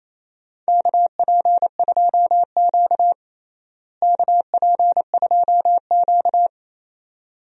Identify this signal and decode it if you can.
Morse code, heard as a keyed tone, about 28 wpm, 700 Hz. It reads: KP2Q KP2Q
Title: DXCC-100 Call Signs - 4 Characters - Repeated 2x - 28wpm